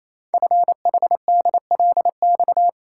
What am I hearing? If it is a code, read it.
Morse code, heard as a keyed tone, about 27 wpm, 700 Hz. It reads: FHDLX
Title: Code Groups - 5 Letter:Numbers - 28wpm